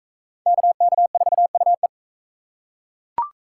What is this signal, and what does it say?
Morse code, heard as a keyed tone, about 39 wpm, 700 Hz. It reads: KK4UE